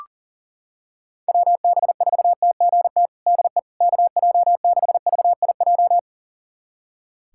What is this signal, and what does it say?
Morse code, heard as a keyed tone, about 37 wpm, 700 Hz. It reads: WB4TGT DE KJ6VIJ
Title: USA Callsign DE Callsign - 6 Characters - 40wpm